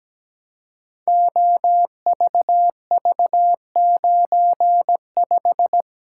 Morse code, heard as a keyed tone, about 17 wpm, 700 Hz. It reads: OVV95
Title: Code Groups - 5 Letter:Numbers - 17wpm